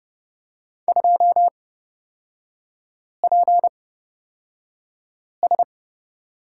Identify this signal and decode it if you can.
Morse code, heard as a keyed tone, about 31 wpm, spaced as for 5 wpm, 690 Hz. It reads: 2PS
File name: course / 30:10wpm / 236 Lesson 38e - Letter q - Three Characters 30wpm